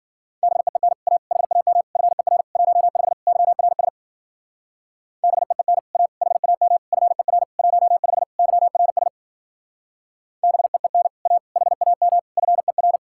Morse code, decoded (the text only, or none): BEEN A HAM FER 15 YRS BEEN A HAM FER 15 YRS BEEN A HAM FER